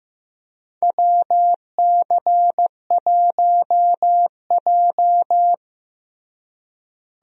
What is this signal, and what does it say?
Morse code, heard as a keyed tone, about 16 wpm, 700 Hz. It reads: WC1J
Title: USA Call Signs - 4 Characters - Repeated 3x - 15wpm